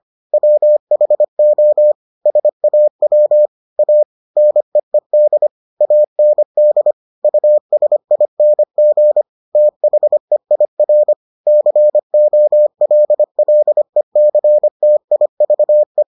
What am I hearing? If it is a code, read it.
Morse code, heard as a keyed tone, about 25 wpm, 600 Hz. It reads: WHO SAW A NEED AND USING THEIR COLLECTIVE